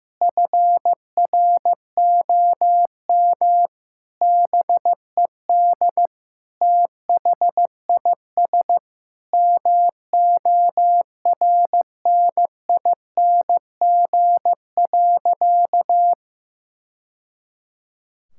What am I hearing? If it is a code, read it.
Morse code, heard as a keyed tone, about 15 wpm, 700 Hz. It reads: FROM BED THIS MORNING.
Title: Sentences from Top 1000 Words - Review Part 6 - 15wpm